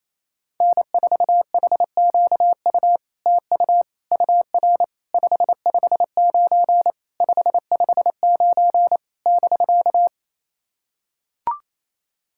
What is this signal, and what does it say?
Morse code, heard as a keyed tone, about 27 wpm, 700 Hz. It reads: N4HQU TU UR 559 559 <BK>
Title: SOTA QSO Practice 28wpm